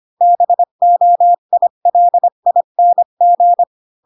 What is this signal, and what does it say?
Morse code, heard as a keyed tone, about 25 wpm, 700 Hz. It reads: BOILING
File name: Pinocchio / Pinocchio ch5 20 25 ik1hge